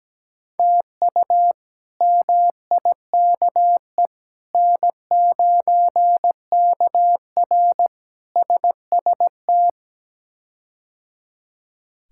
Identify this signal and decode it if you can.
Morse code, heard as a keyed tone, about 17 wpm, 700 Hz. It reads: TU MIKE N9KR SST
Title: K1USN SST QSO Practice - 17wpm